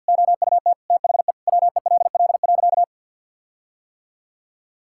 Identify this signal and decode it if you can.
Morse code, heard as a keyed tone, about 51 wpm, 700 Hz. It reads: OUT THE WELL.
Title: Sentences from Top 1600 Words 50wpm